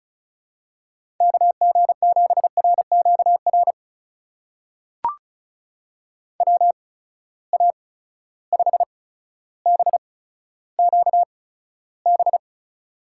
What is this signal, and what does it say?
Morse code, heard as a keyed tone, about 36 wpm, spaced as for 15 wpm, 700 Hz. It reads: KG7RQR WA5BQB
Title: USA Call Signs - 6 Characters - 8x Character Spacing - 35wpm